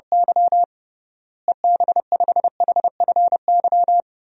Morse code, heard as a keyed tone, about 32 wpm, 700 Hz. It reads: Y EB5HFY
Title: DXCC-100 Call Signs - 6 Characters - Repeated 2x - 30wpm